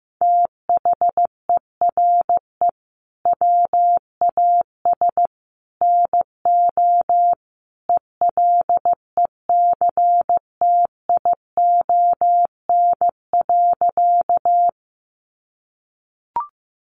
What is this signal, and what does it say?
Morse code, heard as a keyed tone, about 15 wpm, 700 Hz. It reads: THERE WAS NO ELECTION.
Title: Sentences from Top 1500 Words 15wpm